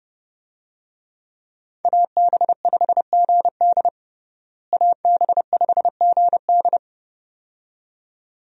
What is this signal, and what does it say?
Morse code, heard as a keyed tone, about 29 wpm, 700 Hz. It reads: AB5GD AB5GD
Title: USA Call Signs - 5 Characters - Repeated 2x - 30wpm